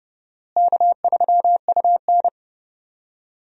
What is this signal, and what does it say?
Morse code, heard as a keyed tone, about 31 wpm, 700 Hz. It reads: K3UN
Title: USA Call Signs - 4 Characters - Repeated 2x - 30wpm